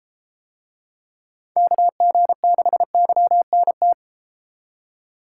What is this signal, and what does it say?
Morse code, heard as a keyed tone, about 32 wpm, 700 Hz. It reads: KG6YNT